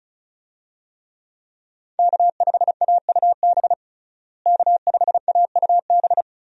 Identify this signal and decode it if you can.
Morse code, heard as a keyed tone, about 36 wpm, 700 Hz. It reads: K5AUB K5AUB
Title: USA Call Signs - 5 Characters - Repeated 2x - 35wpm